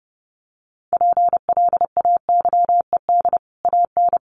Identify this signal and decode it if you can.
Morse code, heard as a keyed tone, about 30 wpm, 700 Hz. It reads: PLAYED AN
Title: Sets of 5 Words Encore2 30wpm